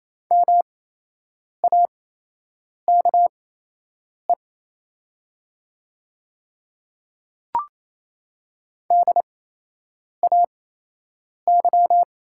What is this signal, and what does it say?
Morse code, heard as a keyed tone, about 28 wpm, spaced as for 8 wpm, 700 Hz. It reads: MAKE DAY
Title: 8x Character Spacing - Top 300 Words - 28wpm